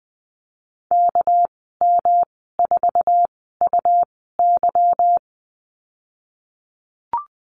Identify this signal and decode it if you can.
Morse code, heard as a keyed tone, about 20 wpm, spaced as for 14 wpm, 700 Hz. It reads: KM4UY
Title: 200 Lesson 32g - Letter k - Call Signs 20wpm